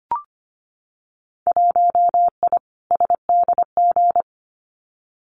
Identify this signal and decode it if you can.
Morse code, heard as a keyed tone, about 25 wpm, 700 Hz. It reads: 1I SDG
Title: Winter Field Day Exchanges 25wpm